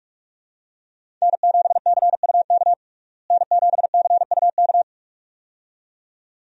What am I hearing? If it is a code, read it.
Morse code, heard as a keyed tone, about 42 wpm, 700 Hz. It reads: N7CUK N7CUK
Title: USA Call Signs - 5 Characters - Repeated 2x - 45wpm